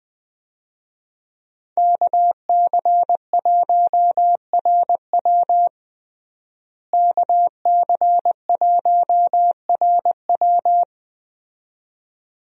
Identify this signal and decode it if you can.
Morse code, heard as a keyed tone, about 20 wpm, 700 Hz. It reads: KC1RW KC1RW
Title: USA Call Signs - 5 Characters - Repeated 2x - 20wpm